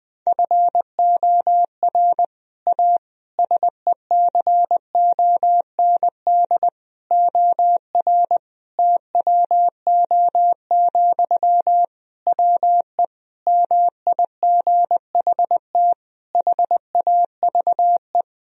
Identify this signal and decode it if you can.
Morse code, heard as a keyed tone, about 20 wpm, 700 Hz. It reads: FOR A SECOND OR TWO, WE MIGHT HAVE
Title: Sentences from Top 1000 Words - Review Part 1 - 20wpm